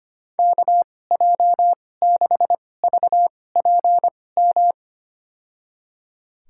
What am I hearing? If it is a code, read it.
Morse code, heard as a keyed tone, about 25 wpm, spaced as for 18 wpm, 700 Hz. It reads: KJ6VPM
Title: USA Call Signs - 6 Characters - 2x Character Spacing - 25wpm